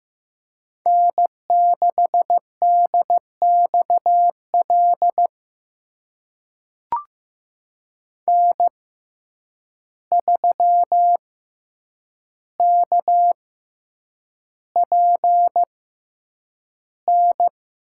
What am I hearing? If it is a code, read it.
Morse code, heard as a keyed tone, about 15 wpm, spaced as for 8 wpm, 700 Hz. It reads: N6DXL N3KPN
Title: USA Call Signs - 6x Character Spacing - 15wpm